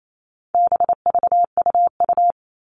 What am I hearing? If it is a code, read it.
Morse code, heard as a keyed tone, about 28 wpm, 700 Hz. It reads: BVUU
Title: All Cut Numbers - 4-Digit 28wpm